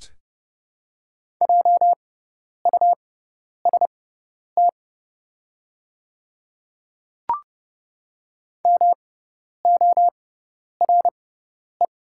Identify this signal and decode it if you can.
Morse code, heard as a keyed tone, about 30 wpm, spaced as for 10 wpm, 690 Hz. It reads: JUST MORE
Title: 6x Character Spacing - Top 200 Words - 30wpm